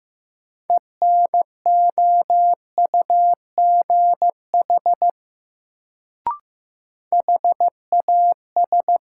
Morse code, heard as a keyed tone, about 15 wpm, 700 Hz. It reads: ENOUGH HAS
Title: Sets of 2 Words - New Word Drill - 15wpm